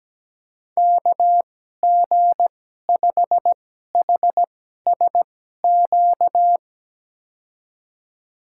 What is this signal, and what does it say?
Morse code, heard as a keyed tone, about 17 wpm, spaced as for 12 wpm, 700 Hz. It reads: KG5HSQ
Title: USA Call Signs - 6 Characters - 2x Character Spacing - 17wpm